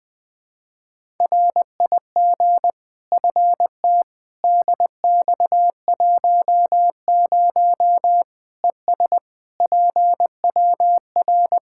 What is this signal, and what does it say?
Morse code, heard as a keyed tone, about 20 wpm, 700 Hz. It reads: RIG FT DX10 ES PWR
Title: K1USN SST QSO Practice 20wpm